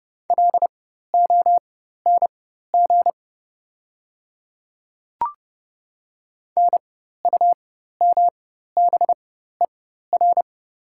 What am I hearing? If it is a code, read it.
Morse code, heard as a keyed tone, about 30 wpm, spaced as for 14 wpm, 690 Hz. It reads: LONG NUMBER
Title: 4x Character Spacing - Top 100 Words - 30wpm